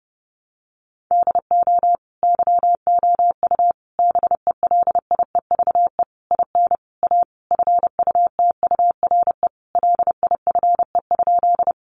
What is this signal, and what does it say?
Morse code, heard as a keyed tone, about 28 wpm, 700 Hz. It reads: DO YOU BELIEVE IN A FUTURE LIFE?
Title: Sentences from Top 1000 Words 30wpm